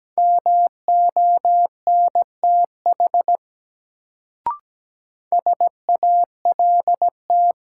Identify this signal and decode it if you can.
Morse code, heard as a keyed tone, about 17 wpm, 700 Hz. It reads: MONTH SALT
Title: Sentences from Top 800 Words - New Word Drill - 17wpm